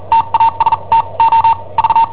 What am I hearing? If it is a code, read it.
Morse code, heard as a keyed tone, about 43 wpm, spaced as for 33 wpm, 910 Hz. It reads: TASTO V
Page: Cassetta esercizi preparata da I7OHP per IW5ELP